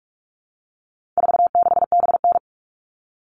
Morse code, heard as a keyed tone, about 44 wpm, 700 Hz. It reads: 46BN
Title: All Cut Numbers - 4-Digit 45wpm